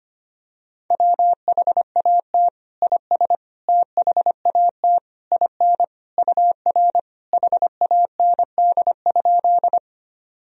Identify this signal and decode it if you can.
Morse code, heard as a keyed tone, about 25 wpm, 700 Hz. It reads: WHAT IS THAT IN UR HAND?
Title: Sentences from Top 200 Words and Abbreviations 25wpm